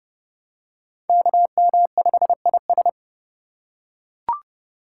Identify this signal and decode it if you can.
Morse code, heard as a keyed tone, about 29 wpm, 700 Hz. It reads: KM5IS